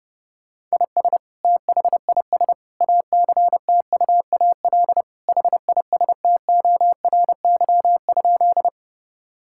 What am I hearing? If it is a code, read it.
Morse code, heard as a keyed tone, about 32 wpm, 700 Hz. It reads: IS THIS ACTUAL HISTORY?
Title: Sentences from Top 1000 Words - Review Part 5 - 30wpm